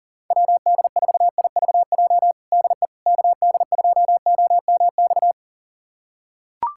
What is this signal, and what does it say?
Morse code, heard as a keyed tone, about 37 wpm, 700 Hz. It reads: WD4IVJ DE KD2OMX
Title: USA Callsign DE Callsign 40wpm